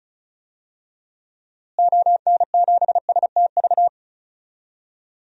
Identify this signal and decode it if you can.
Morse code, heard as a keyed tone, about 36 wpm, 700 Hz. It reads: ON7STV
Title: DXCC-100 Call Signs - 6 Characters - Repeated 2x - 35wpm